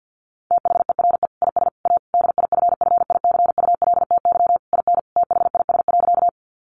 Morse code, heard as a keyed tone, about 50 wpm, 700 Hz. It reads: THERE IS A DIFFICULTY IN THIS.
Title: Sentences from Top 1500 Words 50wpm